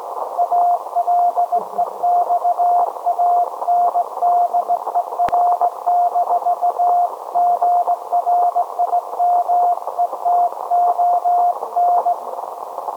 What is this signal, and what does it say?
Morse code, heard as a keyed tone, about 18 wpm, 710 Hz, with radio noise. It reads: ALEXANDER-GRIMETON
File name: storage / message Grimeton SAQ